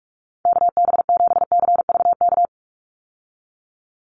Morse code, heard as a keyed tone, about 44 wpm, 700 Hz. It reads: KB7CVK
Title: USA Call Signs 45wpm